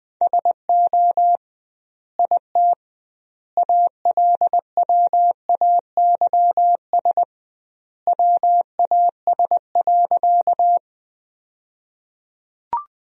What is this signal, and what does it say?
Morse code, heard as a keyed tone, about 21 wpm, 700 Hz. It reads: SO IT ALWAYS WAS.
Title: 2x Word-Spacing - Sentences from Top 300 Words - 20wpm